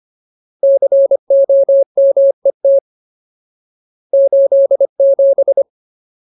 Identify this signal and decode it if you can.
Morse code, heard as a keyed tone, about 25 wpm, 550 Hz. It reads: COMET 87
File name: Ultimate Word List / 25WPM Ultimate